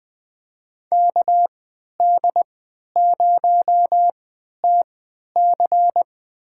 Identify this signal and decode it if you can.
Morse code, heard as a keyed tone, about 20 wpm, spaced as for 11 wpm, 700 Hz. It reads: KD0TC
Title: USA Call Signs - 3x Character Spacing - 20wpm